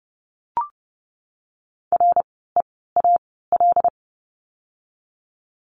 Morse code, heard as a keyed tone, about 30 wpm, spaced as for 17 wpm, 700 Hz. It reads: REAL
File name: Lesson 16f - Letter c - Words 30wpm